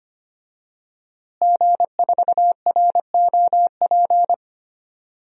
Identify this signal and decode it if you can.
Morse code, heard as a keyed tone, about 25 wpm, 700 Hz. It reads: G4ROP
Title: DXCC-100 Call Signs - 5 Characters - Repeated 2x - 25wpm